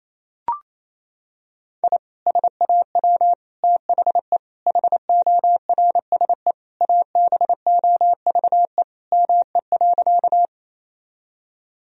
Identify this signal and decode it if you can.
Morse code, heard as a keyed tone, about 28 wpm, 700 Hz. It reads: I SAW THE HORSE ABOVE ME.
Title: Sentences from Top 1000 Words - Review Part 2 - 28wpm